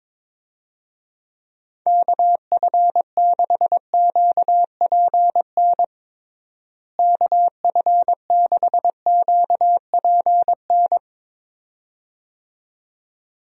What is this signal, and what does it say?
Morse code, heard as a keyed tone, about 22 wpm, 700 Hz. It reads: KF6QPN KF6QPN
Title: USA Call Signs - 6 Characters - Repeated 2x - 22wpm